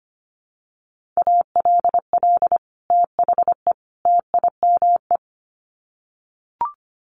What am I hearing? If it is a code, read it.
Morse code, heard as a keyed tone, about 25 wpm, 700 Hz. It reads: ALL THE TIME
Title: Sets of 3 Words 25wpm